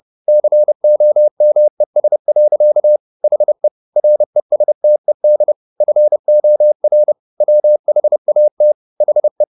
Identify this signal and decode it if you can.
Morse code, heard as a keyed tone, about 30 wpm, 600 Hz. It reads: COMES. HE RESTED FOR WHAT HE